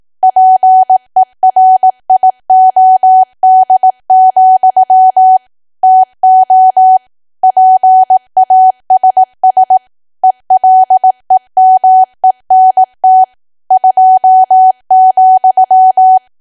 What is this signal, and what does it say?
Morse code, heard as a keyed tone, about 18 wpm, 750 Hz. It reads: PERIOD, TO PASS ELEMENT 2,
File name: cwa / W1AW-Level2 / 18WPM